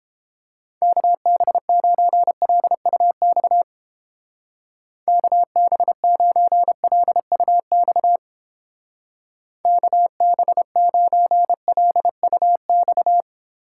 Morse code, heard as a keyed tone, about 31 wpm, 700 Hz. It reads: KB9LUX KB9LUX KB9LUX